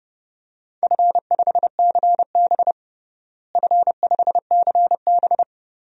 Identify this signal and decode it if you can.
Morse code, heard as a keyed tone, about 30 wpm, 700 Hz. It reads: F5CB F5CB
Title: DXCC-100 Call Signs - 4 Characters - Repeated 2x - 30wpm